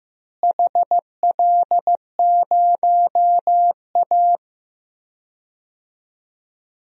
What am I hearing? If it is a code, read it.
Morse code, heard as a keyed tone, about 15 wpm, 700 Hz. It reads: HL0A
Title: DXCC-100 Call Signs - 4 Characters - Repeated 2x - 15wpm